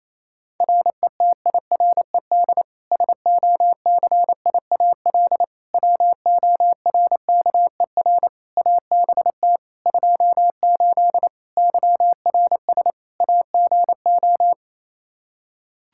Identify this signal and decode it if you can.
Morse code, heard as a keyed tone, about 28 wpm, 700 Hz. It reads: RETIRED SOCIAL WORKER ABT 28 YRS AGO
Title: K1USN SST QSO Practice 28wpm